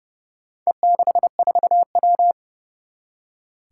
Morse code, heard as a keyed tone, about 28 wpm, 700 Hz. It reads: E64W
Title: Four Challenging Letter-Number Combinations 30wpm